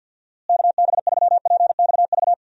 Morse code, heard as a keyed tone, about 50 wpm, 700 Hz. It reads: KB3PXV
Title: USA Call Signs - 6 Characters - Repeated 3x - 50wpm